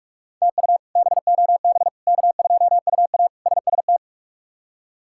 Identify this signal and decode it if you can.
Morse code, heard as a keyed tone, about 46 wpm, 700 Hz. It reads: TU BOB K2UA SST
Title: K1USN SST QSO Practice - 45wpm